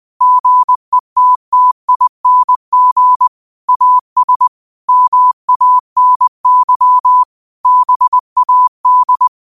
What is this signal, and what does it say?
Morse code, heard as a keyed tone, about 20 wpm, 1000 Hz. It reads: GETTING AS MANY BAD